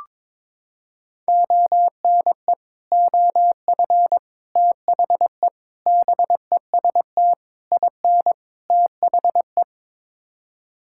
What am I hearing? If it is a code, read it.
Morse code, heard as a keyed tone, about 22 wpm, 700 Hz. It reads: ONE OF THE BEST IN THE
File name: practice / 22wpm / Sets of 6 Words - 22wpm